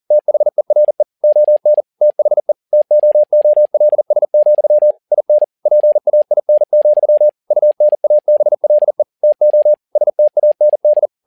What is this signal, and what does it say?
Morse code, heard as a keyed tone, about 40 wpm, 600 Hz. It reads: THERE ON THE TOOLS, IN PAIN, UNABLE TO STAND
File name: learning center / tower trouble / tower touble 40wpm